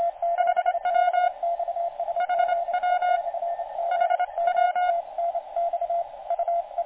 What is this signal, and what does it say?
Morse code, heard as a keyed tone, about 25 wpm, 1440 Hz, with radio noise. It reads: HW HW HW
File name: JA7BXS- -on-80